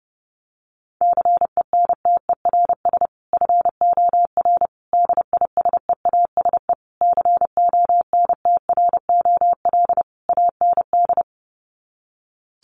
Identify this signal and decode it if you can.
Morse code, heard as a keyed tone, about 30 wpm, 700 Hz. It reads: CENTERS FOR DISEASE CONTROL AND
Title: Sets of 5 Words Encore2 30wpm